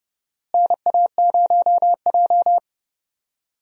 Morse code, heard as a keyed tone, about 32 wpm, 700 Hz. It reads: NA0J
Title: USA Call Signs - 4 Characters - Repeated 2x - 30wpm